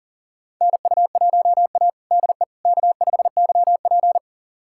Morse code, heard as a keyed tone, about 40 wpm, 700 Hz. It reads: NU1A DE K5YP
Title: USA Callsign DE Callsign - 4 Characters - 40wpm